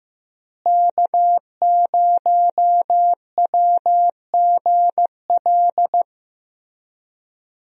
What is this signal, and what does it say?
Morse code, heard as a keyed tone, about 15 wpm, 700 Hz. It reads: K0WGL
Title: USA Call Signs - 5 Characters - Repeated 3x - 15wpm